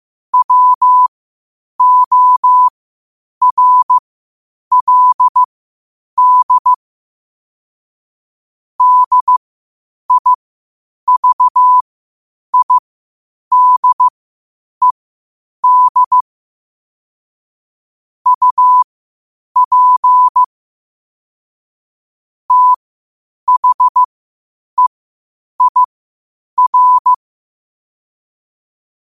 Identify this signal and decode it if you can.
Morse code, heard as a keyed tone, about 15 wpm, spaced as for 8 wpm, 1000 Hz. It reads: WORLD DIVIDED UP THEIR
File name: qotdfiles / 8wpm